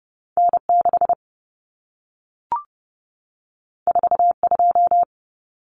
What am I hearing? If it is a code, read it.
Morse code, heard as a keyed tone, about 30 wpm, 700 Hz. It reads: N6 42
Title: Common Cut Numbers - 2-Digit 30wpm